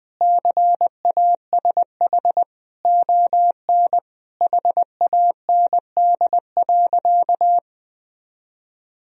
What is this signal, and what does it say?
Morse code, heard as a keyed tone, about 20 wpm, 700 Hz. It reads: CASH ON HAND.
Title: Sentences from Top 1000 Words - Review Part 4 - 20wpm